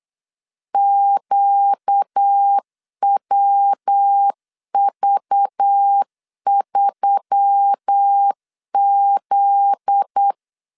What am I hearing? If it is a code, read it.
Morse code, heard as a keyed tone, about 8 wpm, 800 Hz. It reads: QWV3Z
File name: LEZIONI / 7 GF9Z / 7 GF9Z 8wpm 3213